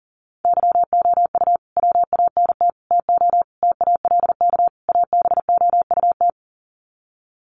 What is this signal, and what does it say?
Morse code, heard as a keyed tone, about 40 wpm, 700 Hz. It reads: YOU WANT TO TALK ABOUT
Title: Sets of 5 Words Encore 40wpm